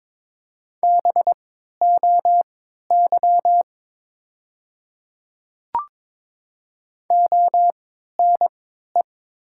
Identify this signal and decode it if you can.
Morse code, heard as a keyed tone, about 22 wpm, spaced as for 12 wpm, 700 Hz. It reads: BOY ONE
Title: 3x Character Spacing - Top 200 Words - 22wpm